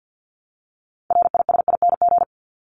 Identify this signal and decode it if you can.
Morse code, heard as a keyed tone, about 50 wpm, 700 Hz. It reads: RISING
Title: ING Suffix - Top 100 Words with Base Words 50wpm